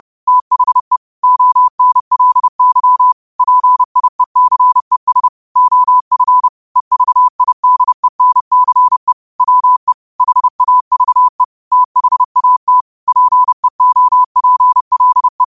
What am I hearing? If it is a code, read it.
Morse code, heard as a keyed tone, about 30 wpm, 1000 Hz. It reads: THE ONLY PIECES OF EVIDENCE WE HAVE THAT PEOPLE
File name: 30wpm